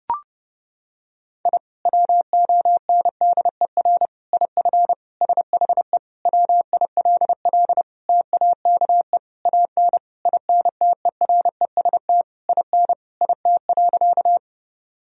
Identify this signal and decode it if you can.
Morse code, heard as a keyed tone, about 32 wpm, 700 Hz. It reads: I WONDER IF SHE WILL TAKE AN INTEREST IN IT.
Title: Sentences from Top 1000 Words - Review Part 3 - 30wpm